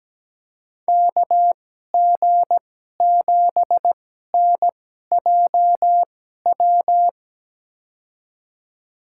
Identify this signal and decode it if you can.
Morse code, heard as a keyed tone, about 17 wpm, spaced as for 12 wpm, 700 Hz. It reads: KG7NJW
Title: USA Call Signs - 6 Characters - 2x Character Spacing - 17wpm